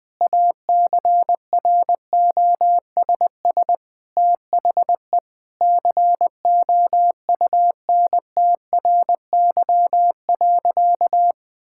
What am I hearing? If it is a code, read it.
Morse code, heard as a keyed tone, about 20 wpm, 700 Hz. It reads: ACROSS THE COUNTRY.
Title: Sentences with 2-Word Phrases - Spoken Beginning - 20wpm